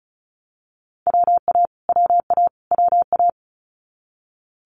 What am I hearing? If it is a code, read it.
Morse code, heard as a keyed tone, about 36 wpm, 700 Hz. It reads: WA WA WA
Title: Multi-letter Phonemes Repeated 3x 35wpm